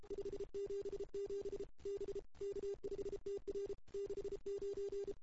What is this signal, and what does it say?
Morse code, heard as a keyed tone, about 31 wpm, 440 Hz. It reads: 577 B K5TR 69
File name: CW / 2008-ss-cw-k5tr-op-n5rz -02:00